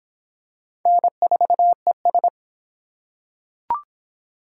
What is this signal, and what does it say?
Morse code, heard as a keyed tone, about 26 wpm, 700 Hz. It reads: N4ES